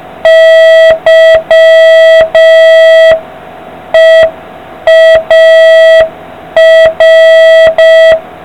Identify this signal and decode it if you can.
Morse code, heard as a keyed tone, about 5 wpm, 650 Hz, with noise. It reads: YEAR